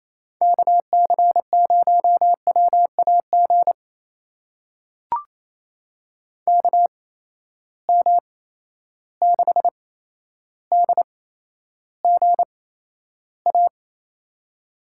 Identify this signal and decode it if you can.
Morse code, heard as a keyed tone, about 28 wpm, spaced as for 12 wpm, 700 Hz. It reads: KC0WAG KM6DGA
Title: USA Call Signs - 6 Characters - 8x Character Spacing - 28wpm